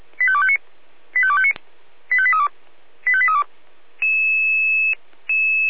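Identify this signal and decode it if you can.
Five-tone selective call (ZVEI), heard as quick runs of five tones, tones of about 70 ms. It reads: 85259 85259 85832 85832
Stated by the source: ZVEI five-tone call 85253 received on SDR.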